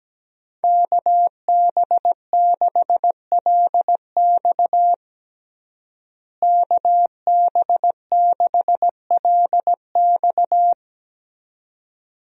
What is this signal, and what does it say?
Morse code, heard as a keyed tone, about 17 wpm, 700 Hz. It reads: KB6LX KB6LX
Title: USA Call Signs - 5 Characters - Repeated 2x - 17wpm